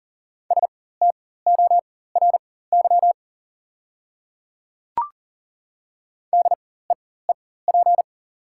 Morse code, heard as a keyed tone, about 38 wpm, spaced as for 18 wpm, 700 Hz. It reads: STORY DEEP
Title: 4x Character Spacing - Top 500 Words - 40wpm